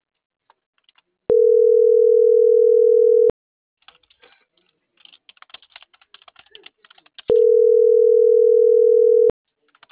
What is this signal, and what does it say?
Ringback tone (North America), 440 + 480 Hz.